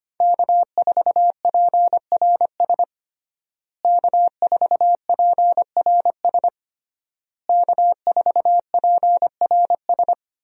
Morse code, heard as a keyed tone, about 25 wpm, 700 Hz. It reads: K4PRS K4PRS K4PRS
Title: USA Call Signs - 5 Characters - Repeated 3x - 25wpm